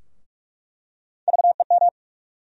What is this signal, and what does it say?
Morse code, heard as a keyed tone, about 46 wpm, 700 Hz. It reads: VEM
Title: USA Call Sign Postfixes - 3 Letters 45wpm